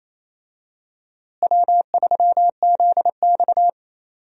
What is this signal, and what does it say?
Morse code, heard as a keyed tone, about 28 wpm, 700 Hz. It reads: W3ZX
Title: USA Call Signs - 4 Characters - Repeated 3x - 28wpm